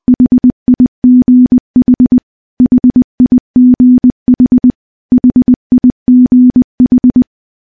Morse code, heard as a keyed tone, about 20 wpm, 250 Hz. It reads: HIGH HIGH HIGH
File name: morse / morse-training-20WPM